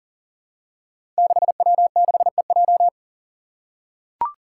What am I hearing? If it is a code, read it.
Morse code, heard as a keyed tone, about 38 wpm, 700 Hz. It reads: 6W6EJ
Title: Five Challenging Letter-Number Combinations 40wpm